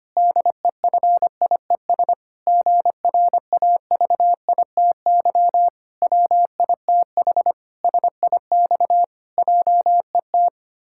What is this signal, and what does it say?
Morse code, heard as a keyed tone, about 25 wpm, 700 Hz. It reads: DEFIES GRAVITY WITH SIX JET
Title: Pangram Challenge v2 - 25wpm